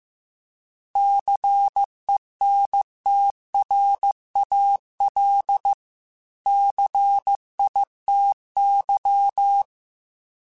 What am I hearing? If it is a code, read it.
Morse code, heard as a keyed tone, about 15 wpm, 800 Hz. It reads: CENTRAL CITY